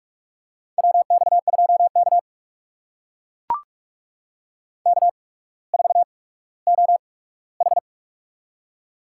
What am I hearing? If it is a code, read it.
Morse code, heard as a keyed tone, about 44 wpm, spaced as for 19 wpm, 700 Hz. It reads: WX2K K4OH
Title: USA Call Signs - 4 Characters - 8x Character Spacing - 45wpm